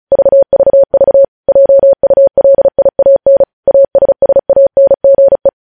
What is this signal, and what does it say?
Morse code, heard as a keyed tone, about 36 wpm, 550 Hz. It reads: VVV JULIAN ASSANGE
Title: News Headlines at 35